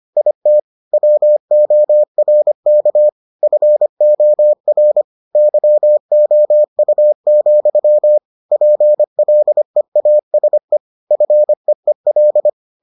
Morse code, heard as a keyed tone, about 25 wpm, 600 Hz. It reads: IT WORK FOR YOU, PLEASE FEEL